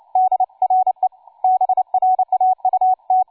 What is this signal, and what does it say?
Morse code, heard as a keyed tone, about 29 wpm, 750 Hz, with radio noise. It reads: D RE BRAUT
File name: cw forum / tu2 f5ldy 18mhz 16h46